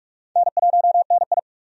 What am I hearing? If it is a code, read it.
Morse code, heard as a keyed tone, about 46 wpm, 700 Hz. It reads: N1NI
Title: USA Call Signs - 4 Characters - 45wpm